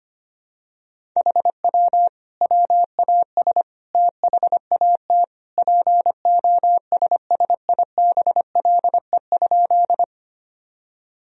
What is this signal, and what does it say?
Morse code, heard as a keyed tone, about 25 wpm, 700 Hz. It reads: HW WAS THAT POSSIBLE?
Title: Sentences from Top 500 Words and Abbreviations 25wpm